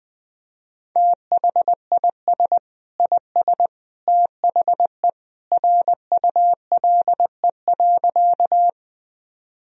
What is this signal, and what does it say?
Morse code, heard as a keyed tone, about 21 wpm, 700 Hz. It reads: THIS IS THE RULE.
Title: Sentences from Top 1000 Words - Review Part 2 - 20wpm